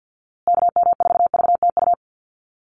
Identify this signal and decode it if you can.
Morse code, heard as a keyed tone, about 50 wpm, 700 Hz. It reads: KM4VTU
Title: USA Call Signs Repeated 2x 50wpm